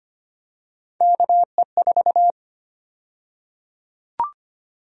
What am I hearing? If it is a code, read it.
Morse code, heard as a keyed tone, about 25 wpm, 700 Hz. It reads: KE4
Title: USA Call Sign Prefixes - 25wpm